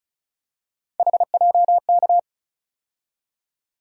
Morse code, heard as a keyed tone, about 36 wpm, 700 Hz. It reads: HJK
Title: USA Call Sign Postfixes - 3 Letters 35wpm